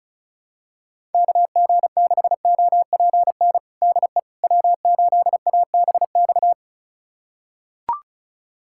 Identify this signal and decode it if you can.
Morse code, heard as a keyed tone, about 36 wpm, 700 Hz. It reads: KG6OPN DE W8ABX
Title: USA Callsign DE Callsign 35wpm